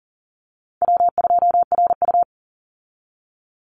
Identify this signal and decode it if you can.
Morse code, heard as a keyed tone, about 40 wpm, 700 Hz. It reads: W2RU
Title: Easy Call Signs Repeated 3x - 40wpm